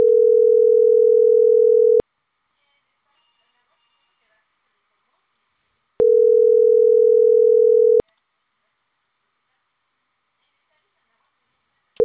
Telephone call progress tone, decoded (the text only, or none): ringback tone